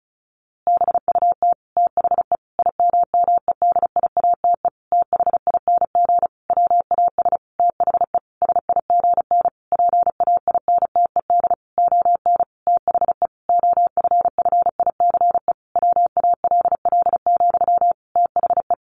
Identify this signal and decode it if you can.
Morse code, heard as a keyed tone, about 34 wpm, 700 Hz. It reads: BUT THE IMMEDIATE THING WAS THE SIGN PAINTED ON THE OFFICE WALL, THE
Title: A Sound Of Thunder 35wpm